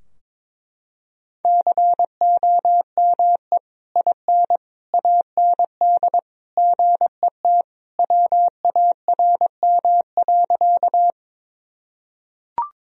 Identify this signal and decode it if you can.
Morse code, heard as a keyed tone, about 21 wpm, 700 Hz. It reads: COME IN AND GET WARM.
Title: Sentences from Top 1000 Words - Review Part 2 - 22wpm